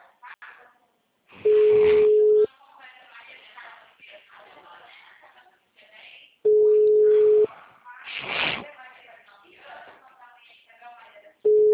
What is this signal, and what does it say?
Ringback tone (Europe, 425 Hz), with speech.
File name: out- -2010- - - .44962